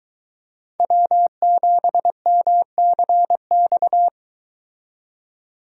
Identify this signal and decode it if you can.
Morse code, heard as a keyed tone, about 23 wpm, 700 Hz. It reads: W7MCX